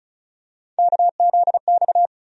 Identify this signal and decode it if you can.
Morse code, heard as a keyed tone, about 36 wpm, 700 Hz. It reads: KZX
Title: USA Call Sign Postfixes - 3 Letters 35wpm